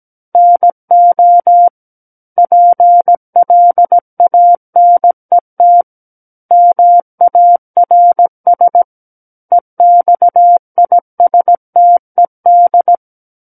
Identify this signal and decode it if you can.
Morse code, heard as a keyed tone, about 17 wpm, 700 Hz. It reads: NO PLANET MARS EXISTED
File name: Book One / 08-Chapter 8 - 17 WPM